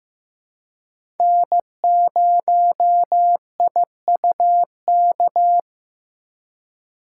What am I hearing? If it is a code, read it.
Morse code, heard as a keyed tone, about 15 wpm, 700 Hz. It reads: N0IUK
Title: USA Call Signs - 5 Characters - 2x Character Spacing - 15wpm